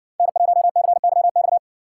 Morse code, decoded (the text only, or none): N1CYX